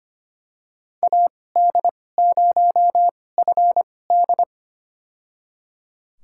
Morse code, heard as a keyed tone, about 25 wpm, spaced as for 18 wpm, 700 Hz. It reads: AD0FD
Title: USA Call Signs - 5 Characters - 2x Character Spacing - 25wpm